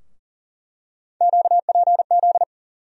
Morse code, heard as a keyed tone, about 40 wpm, 700 Hz. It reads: QPZ
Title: USA Call Sign Postfixes - 3 Letters 40wpm